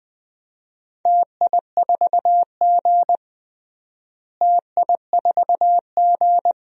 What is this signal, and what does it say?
Morse code, heard as a keyed tone, about 20 wpm, 700 Hz. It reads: TI4G TI4G
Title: DXCC-100 Call Signs - 4 Characters - Repeated 2x - 20wpm